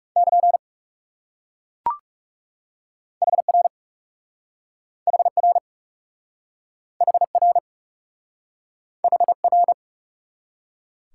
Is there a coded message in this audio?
Yes: Morse code from 0.2 to 9.7 s, heard as a keyed tone, about 38 wpm, 700 Hz. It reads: <KN> HR HR HR HR